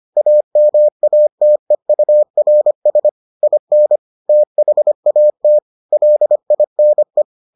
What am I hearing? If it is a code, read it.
Morse code, heard as a keyed tone, about 25 wpm, 600 Hz. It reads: AMATEURS IN THAT LINE